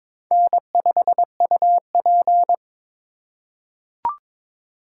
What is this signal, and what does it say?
Morse code, heard as a keyed tone, about 22 wpm, 700 Hz. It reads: N5UP
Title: USA Call Signs - 2x Character Spacing - 22wpm